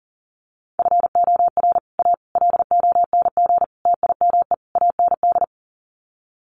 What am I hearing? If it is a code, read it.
Morse code, heard as a keyed tone, about 40 wpm, 700 Hz. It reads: FOR A LONG TIME AND
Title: Sets of 5 Words Encore2 40wpm